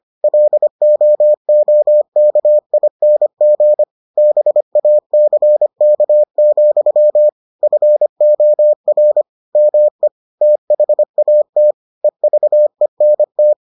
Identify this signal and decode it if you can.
Morse code, heard as a keyed tone, about 25 wpm, 600 Hz. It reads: LOOKING BACK, FOR ME THAT EVENT